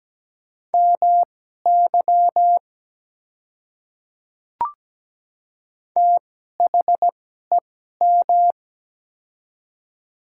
Morse code, heard as a keyed tone, about 17 wpm, spaced as for 12 wpm, 700 Hz. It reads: MY THEM
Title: 2x Character Spacing - Top 100 Words - 17wpm